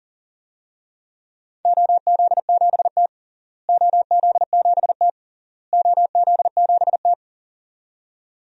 Morse code, heard as a keyed tone, about 40 wpm, 700 Hz. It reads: OZ7T OZ7T OZ7T
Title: DXCC-100 Call Signs - 4 Characters - Repeated 3x - 40wpm